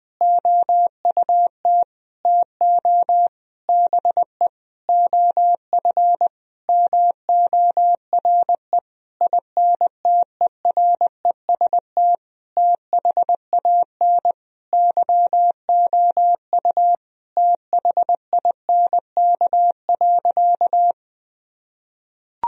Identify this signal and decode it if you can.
Morse code, heard as a keyed tone, about 21 wpm, 700 Hz. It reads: OUT TO BE OF MORE INTEREST THAN YOU THINK.
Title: Sentences from Top 1000 Words - Review Part 2 - 20wpm